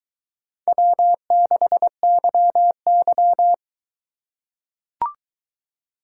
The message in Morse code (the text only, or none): W6YY